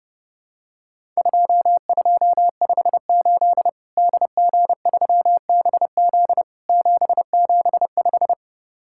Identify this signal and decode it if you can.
Morse code, heard as a keyed tone, about 30 wpm, 700 Hz. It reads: 2258 DG3BZ 775
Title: Worked All Europe DX Contest - QTC Exchange - 30wpm